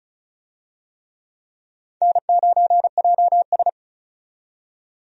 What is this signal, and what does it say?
Morse code, heard as a keyed tone, about 34 wpm, 700 Hz. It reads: N9JS
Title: USA Call Signs - 4 Characters - 35wpm